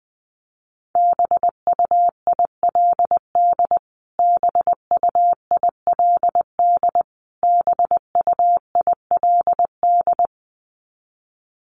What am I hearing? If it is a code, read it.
Morse code, heard as a keyed tone, about 20 wpm, 700 Hz. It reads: BUILD BUILD BUILD
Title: Top 200 Words Repeated 20wpm